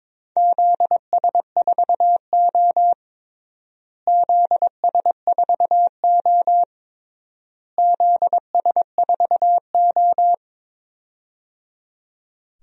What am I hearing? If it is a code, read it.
Morse code, heard as a keyed tone, about 22 wpm, 700 Hz. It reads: ZS4O ZS4O ZS4O
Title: DXCC-100 Call Signs - 4 Characters - Repeated 3x - 22wpm